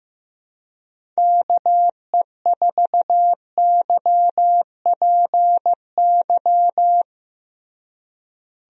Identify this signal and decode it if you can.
Morse code, heard as a keyed tone, about 15 wpm, 700 Hz. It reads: KE4YPY
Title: USA Call Signs - 6 Characters - Repeated 3x - 15wpm